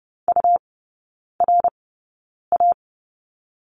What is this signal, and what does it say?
Morse code, heard as a keyed tone, about 30 wpm, spaced as for 9 wpm, 700 Hz. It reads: URA
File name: Lesson 18e - Letter m - Three Characters 30wpm